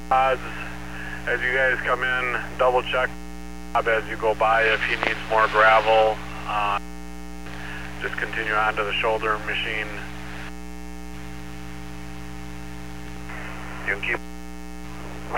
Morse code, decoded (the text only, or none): none